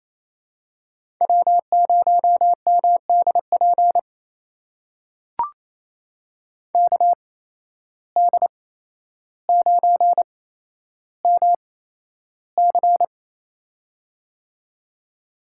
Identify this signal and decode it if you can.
Morse code, heard as a keyed tone, about 28 wpm, spaced as for 12 wpm, 700 Hz. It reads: W0MDP KD9MC
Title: USA Call Signs - 5 Characters - 8x Character Spacing - 28wpm